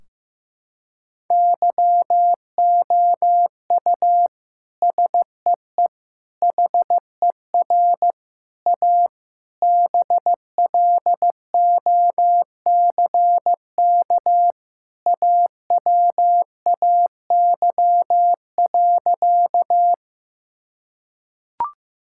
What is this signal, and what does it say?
Morse code, heard as a keyed tone, about 15 wpm, 700 Hz. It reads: YOU SEE HER A BLOCK AWAY.
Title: Sentences from Top 1600 Words 15wpm